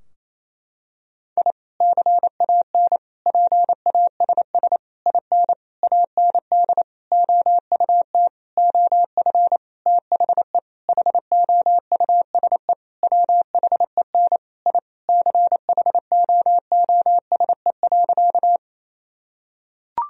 Morse code, heard as a keyed tone, about 28 wpm, 700 Hz. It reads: I CAN PASS IN AND OUT OF THE HOUSE WHEN I CHOOSE.
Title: Sentences from Top 1000 Words - Review Part 4 - 28wpm